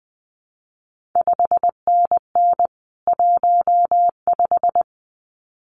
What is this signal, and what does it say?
Morse code, heard as a keyed tone, about 20 wpm, 700 Hz. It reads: 5NN 15
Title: CQ WW DX Contest Exchange - 20wpm